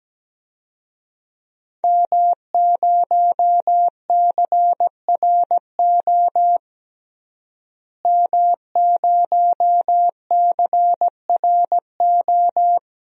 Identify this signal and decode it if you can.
Morse code, heard as a keyed tone, about 17 wpm, 700 Hz. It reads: M0CRO M0CRO
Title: DXCC-100 Call Signs - 5 Characters - Repeated 2x - 17wpm